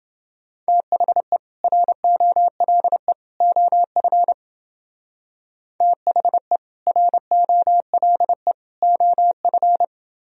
Morse code, heard as a keyed tone, about 29 wpm, 700 Hz. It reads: THE ROLE OF THE ROLE OF